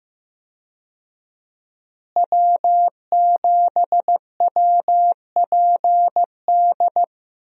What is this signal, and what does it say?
Morse code, heard as a keyed tone, about 15 wpm, 700 Hz. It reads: W7WPD
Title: USA Call Signs - 5 Characters - Repeated 2x - 15wpm